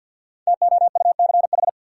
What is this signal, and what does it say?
Morse code, heard as a keyed tone, about 51 wpm, 700 Hz. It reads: TOUCH
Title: Sentences from Top 1200 Words - New Word Drill Alt - 50wpm